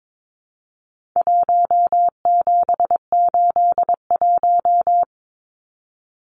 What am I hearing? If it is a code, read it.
Morse code, heard as a keyed tone, about 22 wpm, 700 Hz. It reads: 1781
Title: Numbers - Contest Style - 4-Digit 22wpm